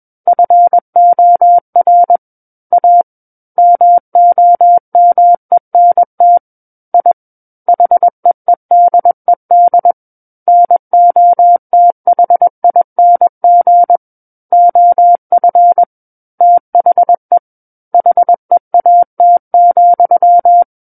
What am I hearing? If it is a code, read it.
Morse code, heard as a keyed tone, about 21 wpm, 700 Hz. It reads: FOR A MOMENT I HEEDED NOTHING OF THE HEAT,
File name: Book One / 12-Chapter 12 - 21 WPM